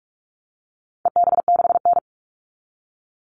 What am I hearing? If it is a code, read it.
Morse code, heard as a keyed tone, about 44 wpm, 700 Hz. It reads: EB6N
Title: All Cut Numbers - 4-Digit 45wpm